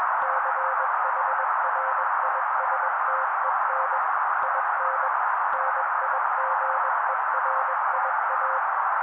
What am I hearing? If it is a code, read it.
Morse code, heard as a keyed tone, about 21 wpm, 530 Hz. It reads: CHRISTEN IN NIGERIA
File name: daily / 20wpm noise